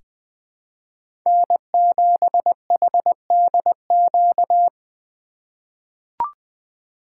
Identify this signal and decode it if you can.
Morse code, heard as a keyed tone, about 20 wpm, 700 Hz. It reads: N7HDQ
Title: USA Call Signs - 5 Characters - 8x Character Spacing - 20wpm